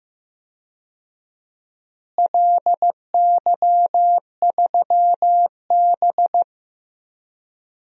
Morse code, heard as a keyed tone, about 15 wpm, 700 Hz. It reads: LY3B
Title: DXCC-100 Call Signs - 4 Characters - Repeated 3x - 15wpm